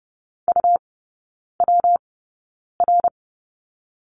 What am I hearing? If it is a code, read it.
Morse code, heard as a keyed tone, about 30 wpm, spaced as for 9 wpm, 700 Hz. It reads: UWR
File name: Lesson 19e - Letter w - Three Characters 30wpm